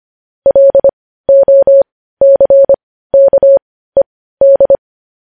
Morse code, heard as a keyed tone, about 25 wpm, spaced as for 15 wpm, 550 Hz. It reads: LOCKED
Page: news headlines in morse code at 15 wpm